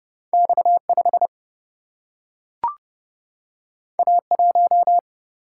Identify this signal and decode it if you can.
Morse code, heard as a keyed tone, about 30 wpm, 700 Hz. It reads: X5 A1
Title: Single-Letter Single-Number 30wpm